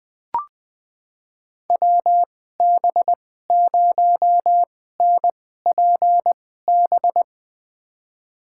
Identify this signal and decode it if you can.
Morse code, heard as a keyed tone, about 20 wpm, spaced as for 15 wpm, 700 Hz. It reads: WB0NPB
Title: USA Call Signs - 6 Characters - 2x Character Spacing - 20wpm